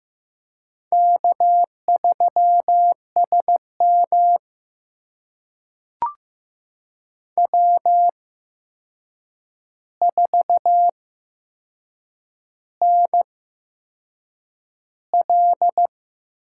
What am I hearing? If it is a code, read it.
Morse code, heard as a keyed tone, about 15 wpm, spaced as for 6 wpm, 700 Hz. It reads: K3SM W4NL
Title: USA Call Signs - 8x Character Spacing - 15wpm